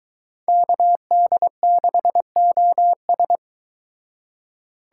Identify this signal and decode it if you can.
Morse code, heard as a keyed tone, about 23 wpm, 700 Hz. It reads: KD6OS